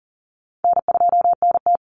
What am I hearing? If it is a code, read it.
Morse code, heard as a keyed tone, about 40 wpm, 700 Hz. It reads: N2NT
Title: Call Signs Repeated 2x - 40wpm